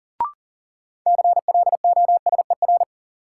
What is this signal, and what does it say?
Morse code, heard as a keyed tone, about 40 wpm, 700 Hz. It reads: CLOSER
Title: Sets of 5 Words Encore - New Word Drill - 40wpm